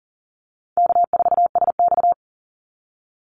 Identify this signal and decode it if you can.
Morse code, heard as a keyed tone, about 40 wpm, 700 Hz. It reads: K4SX
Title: Call Signs Repeated 2x - 40wpm